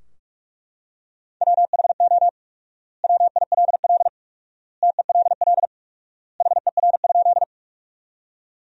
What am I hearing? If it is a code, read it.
Morse code, heard as a keyed tone, about 41 wpm, 700 Hz. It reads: WHO WILL TELL HER?
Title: 4x Word-Spacing - Sentences from Top 200 Words - 45wpm